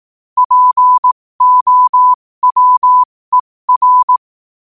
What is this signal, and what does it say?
Morse code, heard as a keyed tone, about 18 wpm, spaced as for 15 wpm, 1000 Hz. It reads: POWER
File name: qotdfiles / 15wpm